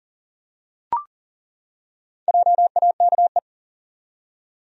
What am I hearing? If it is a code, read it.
Morse code, heard as a keyed tone, about 40 wpm, 700 Hz. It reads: JAKE
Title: USA Common Names - 40wpm